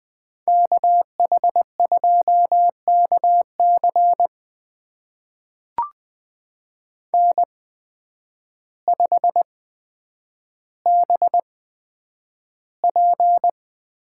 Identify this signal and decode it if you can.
Morse code, heard as a keyed tone, about 20 wpm, spaced as for 9 wpm, 700 Hz. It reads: KH2KC N5BP
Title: USA Call Signs - 5 Characters - 8x Character Spacing - 20wpm